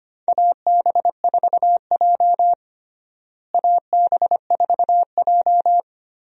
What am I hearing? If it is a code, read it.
Morse code, heard as a keyed tone, about 25 wpm, 700 Hz. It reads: AB4J AB4J
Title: USA Call Signs - 4 Characters - Repeated 3x - 25wpm